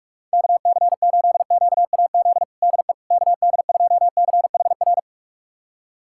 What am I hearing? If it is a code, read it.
Morse code, heard as a keyed tone, about 41 wpm, 700 Hz. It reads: KC8QAZ DE KD2CHR
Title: USA Callsign DE Callsign - 6 Characters - 45wpm